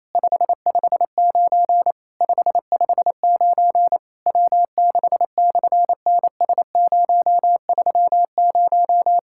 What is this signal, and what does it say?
Morse code, heard as a keyed tone, about 27 wpm, 700 Hz. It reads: 559 559 W6/NS030
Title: SOTA QSO Practice 28wpm